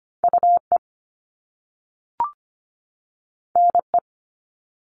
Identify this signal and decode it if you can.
Morse code, heard as a keyed tone, about 25 wpm, 700 Hz. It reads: UE NE
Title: Top Two Letter Combinations in Words 25wpm